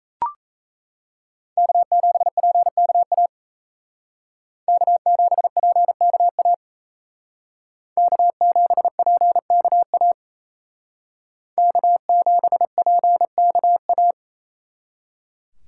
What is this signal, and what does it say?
Morse code, heard as a keyed tone, about 35 wpm, 700 Hz. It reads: K7PKA K7PKA K7PKA K7PKA